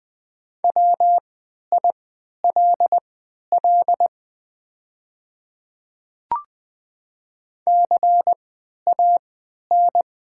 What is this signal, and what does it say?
Morse code, heard as a keyed tone, about 20 wpm, spaced as for 11 wpm, 700 Hz. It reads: WILL CAN
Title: 3x Character Spacing - Top 300 Words - 20wpm